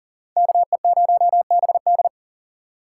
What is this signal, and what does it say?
Morse code, heard as a keyed tone, about 51 wpm, 700 Hz. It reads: KE0BD